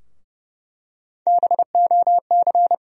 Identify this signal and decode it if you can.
Morse code, heard as a keyed tone, about 32 wpm, 700 Hz. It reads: BOC
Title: USA Call Sign Postfixes - 3 Letters 30wpm